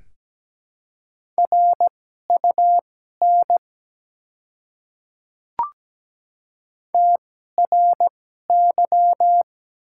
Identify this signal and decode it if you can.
Morse code, heard as a keyed tone, about 17 wpm, spaced as for 12 wpm, 700 Hz. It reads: RUN TRY
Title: 2x Character Spacing - Top 300 Words - 17wpm